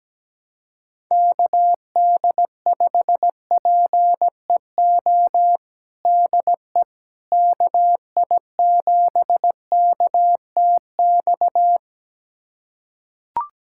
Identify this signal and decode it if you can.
Morse code, heard as a keyed tone, about 17 wpm, 700 Hz. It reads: KD5PEO DE KI7KTX
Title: USA Callsign DE Callsign - 6 Characters - 17wpm